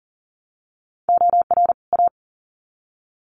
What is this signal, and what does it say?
Morse code, heard as a keyed tone, about 40 wpm, 700 Hz. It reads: OR A
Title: Sets of 2 Words 40wpm